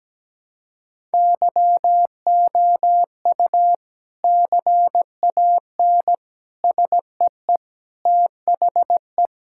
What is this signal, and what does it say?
Morse code, heard as a keyed tone, about 17 wpm, 700 Hz. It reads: YOU CAN SEE THE
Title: Sentences from Top 1000 Words - Review Part 5 - 17wpm